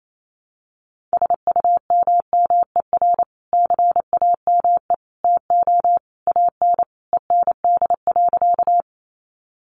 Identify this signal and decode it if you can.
Morse code, heard as a keyed tone, about 28 wpm, 700 Hz. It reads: SUMMER CAME TO AN END.
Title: Sentences from Top 1100 Words 28wpm